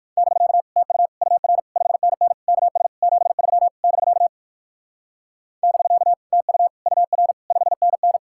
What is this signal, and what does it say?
Morse code, heard as a keyed tone, about 64 wpm, spaced as for 56 wpm, 700 Hz. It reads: <BK> TU UR 5NN KS 73 <BK> <BK> TU UR 5NN